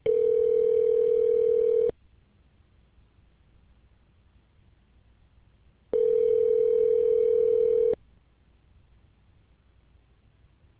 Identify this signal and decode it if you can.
Ringback tone (North America), 440 + 480 Hz.